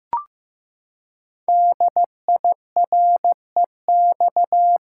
Morse code, heard as a keyed tone, about 15 wpm, 700 Hz. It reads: DIREX
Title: Code Groups - 5 Letters - 15wpm